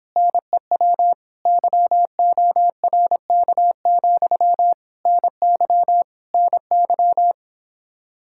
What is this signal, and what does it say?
Morse code, heard as a keyed tone, about 26 wpm, 700 Hz. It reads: NEW YORK, NY NY